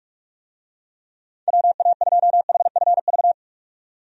Morse code, heard as a keyed tone, about 42 wpm, 700 Hz. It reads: WA2HFV
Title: USA Call Signs - 6 Characters - 45wpm